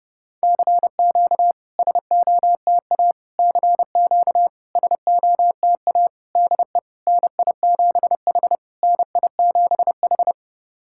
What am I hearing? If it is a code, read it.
Morse code, heard as a keyed tone, about 29 wpm, 690 Hz. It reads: CQ SOTA CQ SOTA DE NI7H NI7H
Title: SOTA QSO Practice 30wpm